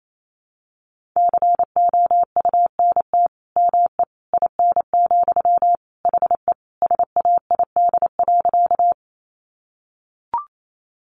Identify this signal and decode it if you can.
Morse code, heard as a keyed tone, about 28 wpm, 700 Hz. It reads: COUNT ME IN, HE SAID.
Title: Sentences from Top 1500 Words 28wpm